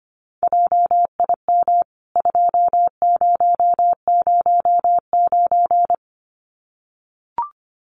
Morse code, heard as a keyed tone, about 25 wpm, 700 Hz. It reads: JIM 2009
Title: CWT Exchange 25wpm